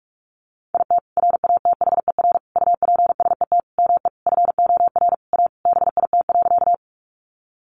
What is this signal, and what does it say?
Morse code, heard as a keyed tone, about 44 wpm, 700 Hz. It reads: IT RATHER UPSET ME FOR A BIT.
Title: Sentences from Top 900 Words 45wpm